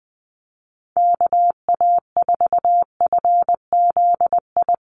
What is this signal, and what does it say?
Morse code, heard as a keyed tone, about 20 wpm, 700 Hz. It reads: KA4FZI
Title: USA Call Signs Repeated 2x 20wpm